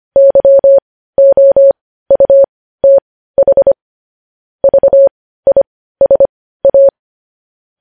Morse code, heard as a keyed tone, about 25 wpm, spaced as for 15 wpm, 550 Hz. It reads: YOUTH VISA
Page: News Headlines in Morse Code at 15 WPM